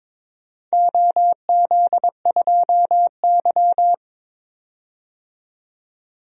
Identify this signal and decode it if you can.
Morse code, heard as a keyed tone, about 21 wpm, 700 Hz. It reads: OZ2Y
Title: DXCC-100 Call Signs - 4 Characters - 22wpm